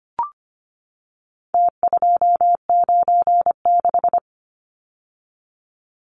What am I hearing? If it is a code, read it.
Morse code, heard as a keyed tone, about 25 wpm, 700 Hz. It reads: T296
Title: Numbers - Contest Style - 4-Digit 25wpm